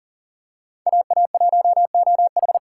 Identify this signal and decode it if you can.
Morse code, heard as a keyed tone, about 40 wpm, 700 Hz. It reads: AA1OH
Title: USA Call Signs - 5 Characters - 3x Character Spacing - 40wpm